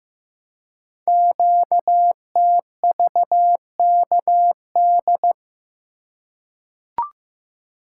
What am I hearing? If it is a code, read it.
Morse code, heard as a keyed tone, about 15 wpm, 700 Hz. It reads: QTVKD
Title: Code Groups - 5 Letter:Numbers - 15wpm